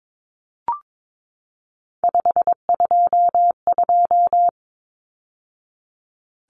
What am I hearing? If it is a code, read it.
Morse code, heard as a keyed tone, about 22 wpm, 700 Hz. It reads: 522
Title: Numbers - 3-Digit 22wpm